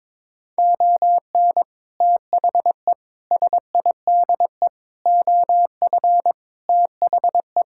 Morse code, heard as a keyed tone, about 22 wpm, 700 Hz. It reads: ON THE SIDE OF THE
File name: practice / 22wpm / Sentences with 3-Word Phrases - 22wpm